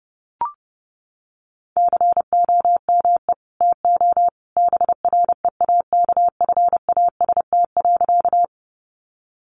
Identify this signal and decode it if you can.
Morse code, heard as a keyed tone, about 30 wpm, 700 Hz. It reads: COME TO BREAKFAST.
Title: Sentences from Top 1500 Words 30wpm